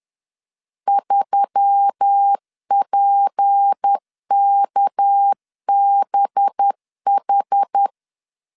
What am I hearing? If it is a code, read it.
Morse code, heard as a keyed tone, about 10 wpm, 800 Hz. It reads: 3PKBH
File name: CW / DET / 1 6 1500C S B 10wpm 4137